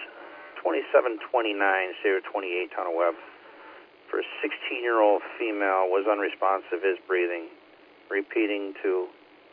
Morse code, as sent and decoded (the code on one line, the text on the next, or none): none